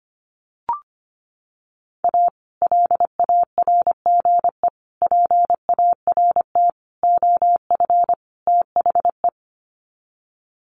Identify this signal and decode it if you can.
Morse code, heard as a keyed tone, about 25 wpm, 700 Hz. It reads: A LARGE PART OF THE
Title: Sets of 5 Words Encore2 25wpm